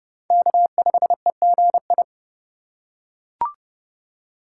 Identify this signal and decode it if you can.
Morse code, heard as a keyed tone, about 30 wpm, 700 Hz. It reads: K5EGI
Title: USA Call Signs - 4x Character Spacing - 30wpm